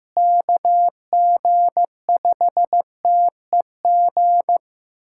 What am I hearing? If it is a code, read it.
Morse code, heard as a keyed tone, about 15 wpm, 700 Hz. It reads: KG5TEG
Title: USA Callsign DE Callsign 15wpm